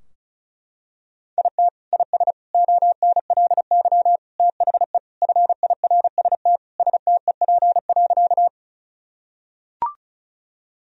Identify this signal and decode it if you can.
Morse code, heard as a keyed tone, about 36 wpm, 700 Hz. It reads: IT IS ONLY THE FIRST STEP.
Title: Sentences from Top 1000 Words - Review Part 2 - 35wpm